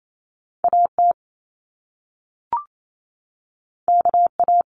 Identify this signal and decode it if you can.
Morse code, heard as a keyed tone, about 28 wpm, 700 Hz. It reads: AT KA
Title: Top Two Letter Combinations in Words 28wpm